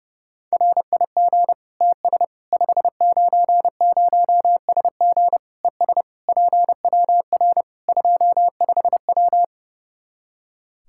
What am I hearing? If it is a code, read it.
Morse code, heard as a keyed tone, about 32 wpm, 700 Hz. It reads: RIG TS 590SG ES PWR 25W
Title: K1USN SST QSO Practice 30wpm